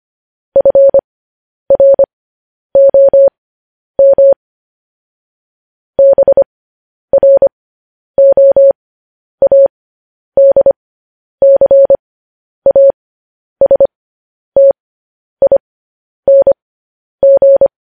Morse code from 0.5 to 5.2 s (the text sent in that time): FROM